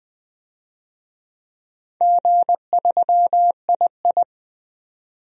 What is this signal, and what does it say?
Morse code, heard as a keyed tone, about 20 wpm, 700 Hz. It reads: G3II
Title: DXCC-100 Call Signs - 4 Characters - 20wpm